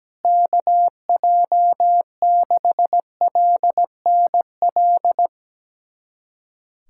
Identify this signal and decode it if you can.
Morse code, heard as a keyed tone, about 17 wpm, 700 Hz. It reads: KJ6LNL
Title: USA Callsign DE Callsign - 6 Characters - 17wpm